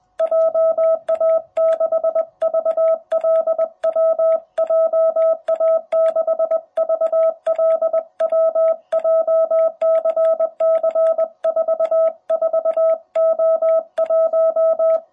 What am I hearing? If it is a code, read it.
Morse code, heard as a keyed tone, about 20 wpm, 660 Hz. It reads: JA6VLW JA6VLW JCC44O1
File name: jh8cbh / 2021cw